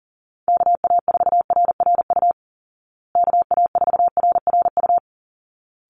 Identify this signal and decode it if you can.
Morse code, heard as a keyed tone, about 40 wpm, 700 Hz. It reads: KA4RRU KA4RRU
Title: Call Signs Repeated 3x - 40wpm